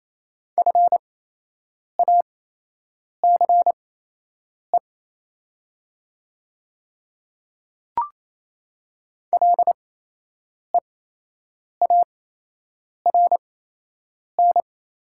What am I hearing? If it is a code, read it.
Morse code, heard as a keyed tone, about 28 wpm, spaced as for 8 wpm, 700 Hz. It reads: FACE LEARN
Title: 8x Character Spacing - Top 300 Words - 28wpm